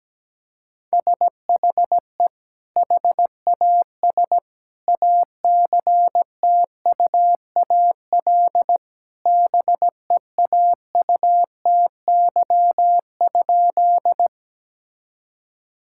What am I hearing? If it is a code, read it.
Morse code, heard as a keyed tone, about 17 wpm, 700 Hz. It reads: SHE HAS ACTUAL BEAUTY?
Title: Sentences from Top 1000 Words - Review Part 5 - 17wpm